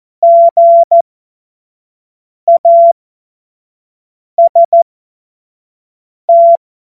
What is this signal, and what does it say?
Morse code, heard as a keyed tone, about 14 wpm, spaced as for 5 wpm, 680 Hz. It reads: GAST
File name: Morsekurs / Witze / Witz 29. Juni 2021, 25 F70